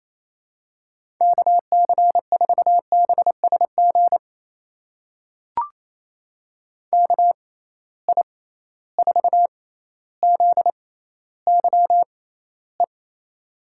Morse code, heard as a keyed tone, about 27 wpm, spaced as for 14 wpm, 700 Hz. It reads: KC4BSG KI4ZYE
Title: USA Call Signs - 6 Characters - 6x Character Spacing - 28wpm